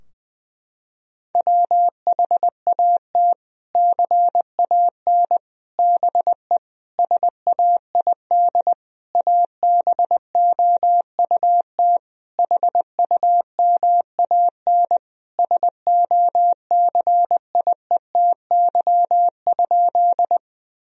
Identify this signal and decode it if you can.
Morse code, heard as a keyed tone, about 20 wpm, 700 Hz. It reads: WHAT CAN BE SAID ABOUT HUMAN SOCIETY?
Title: Sentences from Top 1000 Words - Review Part 4 - 20wpm